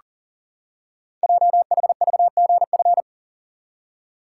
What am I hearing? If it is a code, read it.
Morse code, heard as a keyed tone, about 39 wpm, 700 Hz. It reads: JHVGF
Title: Code Groups - 5 Letters - 40wpm